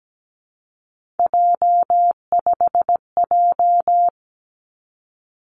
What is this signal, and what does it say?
Morse code, heard as a keyed tone, about 17 wpm, 700 Hz. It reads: J5J
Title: Three Challenging Letter-Number Combinations 17wpm